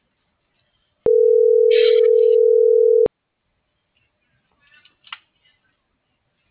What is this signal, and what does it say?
Ringback tone (North America), 440 + 480 Hz.